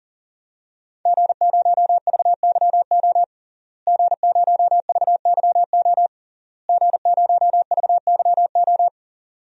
Morse code, heard as a keyed tone, about 40 wpm, 700 Hz. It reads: G0VYO G0VYO G0VYO
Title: DXCC-100 Call Signs - 5 Characters - Repeated 3x - 40wpm